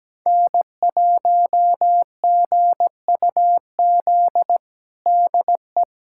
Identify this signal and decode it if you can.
Morse code, heard as a keyed tone, about 17 wpm, 700 Hz. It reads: N1GUZ DE
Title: USA Callsign DE Callsign 17wpm